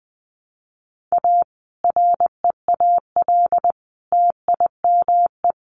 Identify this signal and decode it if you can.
Morse code, heard as a keyed tone, about 20 wpm, 700 Hz. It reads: A REAL TIME
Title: A Sound Of Thunder 20wpm